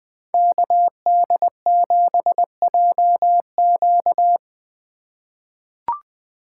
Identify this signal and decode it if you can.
Morse code, heard as a keyed tone, about 20 wpm, 700 Hz. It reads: KD7JQ
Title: USA Call Signs - 5 Characters - 4x Character Spacing - 20wpm